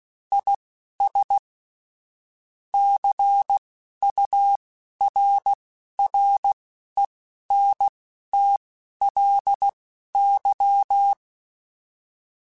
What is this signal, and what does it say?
Morse code, heard as a keyed tone, about 16 wpm, spaced as for 12 wpm, 800 Hz. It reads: IS CURRENTLY
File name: Spaced Characters / Spaced Characters-mp3s / Weather 5 12 sp @15